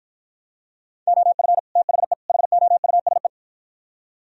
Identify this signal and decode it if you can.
Morse code, heard as a keyed tone, about 51 wpm, 700 Hz. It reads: OF THE HOUSE